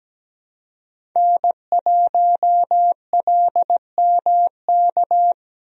Morse code, heard as a keyed tone, about 17 wpm, 700 Hz. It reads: N1LMK
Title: USA Call Signs - 5 Characters - Repeated 2x - 17wpm